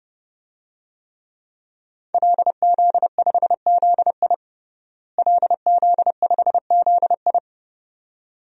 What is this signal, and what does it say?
Morse code, heard as a keyed tone, about 30 wpm, 700 Hz. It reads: LZ5ZI LZ5ZI
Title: DXCC-100 Call Signs - 5 Characters - Repeated 2x - 30wpm